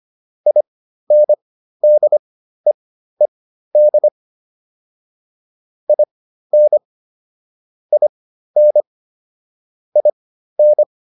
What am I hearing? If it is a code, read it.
Morse code, heard as a keyed tone, about 25 wpm, spaced as for 13 wpm, 600 Hz. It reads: INDEED IN IN IN